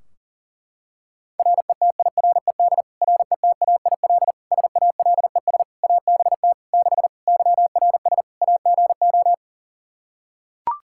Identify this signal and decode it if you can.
Morse code, heard as a keyed tone, about 40 wpm, 700 Hz. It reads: RETIRED RETAIL SALES ABT 6 YRS AGO
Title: Practice QSO - Exchange 3 - More Info - 40wpm